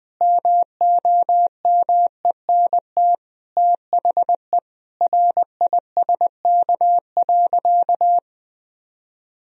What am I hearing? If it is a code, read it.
Morse code, heard as a keyed tone, about 19 wpm, 700 Hz. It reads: MOMENT THE RISK.
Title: Sentences from Top 1000 Words - Review Part 4 - 20wpm